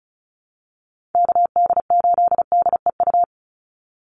Morse code, heard as a keyed tone, about 34 wpm, 700 Hz. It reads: KD8DEU
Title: NA SOTA Chaser Callsigns 35wpm